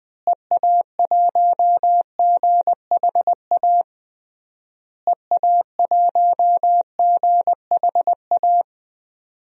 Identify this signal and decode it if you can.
Morse code, heard as a keyed tone, about 20 wpm, 700 Hz. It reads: EA1GHA EA1GHA
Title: DXCC-100 Call Signs - 6 Characters - Repeated 3x - 20wpm